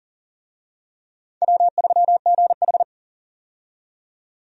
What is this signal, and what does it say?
Morse code, heard as a keyed tone, about 40 wpm, 690 Hz. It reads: W3GH
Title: USA Call Signs - 4 Characters - 40wpm